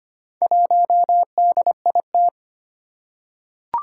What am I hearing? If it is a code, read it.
Morse code, heard as a keyed tone, about 25 wpm, 700 Hz. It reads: 1DIT
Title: Four Challenging Letter-Number Combinations 25wpm